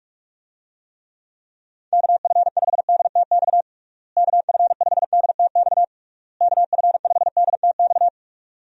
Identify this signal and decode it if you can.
Morse code, heard as a keyed tone, about 46 wpm, 700 Hz. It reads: KF5DTX KF5DTX KF5DTX
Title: USA Call Signs - 6 Characters - Repeated 3x - 45wpm